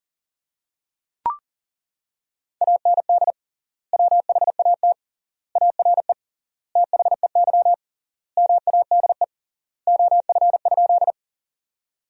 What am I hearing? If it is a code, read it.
Morse code, heard as a keyed tone, about 40 wpm, 700 Hz. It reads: AND WHAT ARE THEY MADE OF?
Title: 3x Word-Spacing - Sentences from Top 200 Words - 40wpm